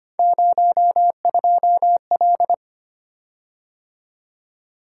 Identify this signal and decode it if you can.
Morse code, heard as a keyed tone, about 25 wpm, 700 Hz. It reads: 02L
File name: Three Letter-Number Combinations - 25wpm